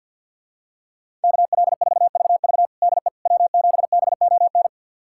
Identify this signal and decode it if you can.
Morse code, heard as a keyed tone, about 51 wpm, 700 Hz. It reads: KL4VV DE W7BON
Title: USA Callsign DE Callsign - 5 Characters - 50wpm